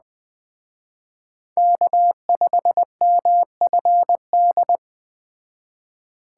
Morse code, heard as a keyed tone, about 21 wpm, 700 Hz. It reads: K5MFD